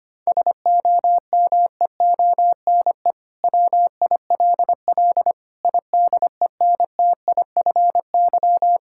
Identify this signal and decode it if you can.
Morse code, heard as a keyed tone, about 25 wpm, 700 Hz. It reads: SOMEONE WILL IDENTIFY
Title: Sentences from Top 1000 Words - Review Part 6 - 25wpm